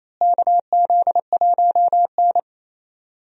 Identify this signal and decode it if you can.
Morse code, heard as a keyed tone, about 28 wpm, 700 Hz. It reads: KZ1N
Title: USA Call Signs - 4 Characters - Repeated 2x - 28wpm